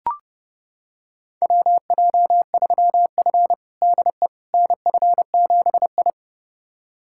Morse code, heard as a keyed tone, about 32 wpm, 700 Hz. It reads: WJ3F DE NF7I
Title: USA Callsign DE Callsign - 4 Characters - 30wpm